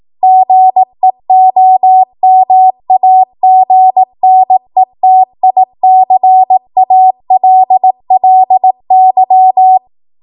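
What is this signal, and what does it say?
Morse code, heard as a keyed tone, about 18 wpm, 750 Hz. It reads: GEOMAGNETICALLY
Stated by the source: Code practice at 18 WPM, a transition run bridging 15 to 20 WPM.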